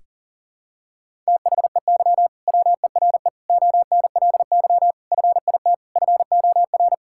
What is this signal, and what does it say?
Morse code, heard as a keyed tone, about 40 wpm, 700 Hz. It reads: THEY WERE ONLY FIT FOR
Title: Sentences from Top 1000 Words - Review Part 5 - 40wpm